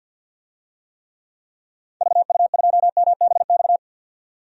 Morse code, heard as a keyed tone, about 48 wpm, 700 Hz. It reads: VU2MBX
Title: DXCC-100 Call Signs - 6 Characters - 50wpm